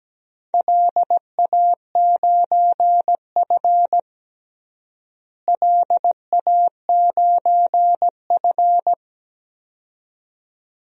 Morse code, heard as a keyed tone, about 17 wpm, 700 Hz. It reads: LA9F LA9F
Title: DXCC-100 Call Signs - 4 Characters - Repeated 2x - 17wpm